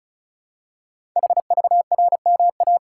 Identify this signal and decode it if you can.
Morse code, heard as a keyed tone, about 36 wpm, 690 Hz. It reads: HVRMA
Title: Code Groups - 5 Letters - 35wpm